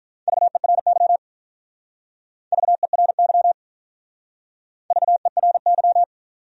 Morse code, heard as a keyed tone, about 46 wpm, 700 Hz. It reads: VERY VERY VERY